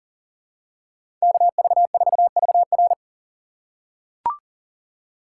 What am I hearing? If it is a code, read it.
Morse code, heard as a keyed tone, about 38 wpm, 700 Hz. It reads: KV4VR
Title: USA Call Signs - 5 Characters - 4x Character Spacing - 40wpm